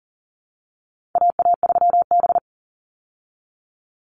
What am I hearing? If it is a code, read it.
Morse code, heard as a keyed tone, about 40 wpm, 700 Hz. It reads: AA3B
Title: Call Signs Repeated 2x - 40wpm